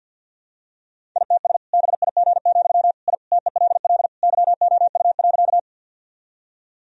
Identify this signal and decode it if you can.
Morse code, heard as a keyed tone, about 51 wpm, 700 Hz. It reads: ITS BIG, I TELL YOU.
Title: Sentences from Top 1000 Words - Review Part 1 - 50wpm